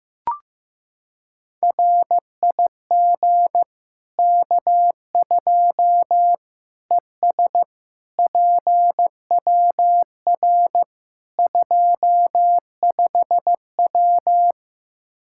Morse code, heard as a keyed tone, about 15 wpm, 700 Hz. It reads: RIG K2 ES PWR 25W
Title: Practice QSO - Exchange 2 - Info - 15wpm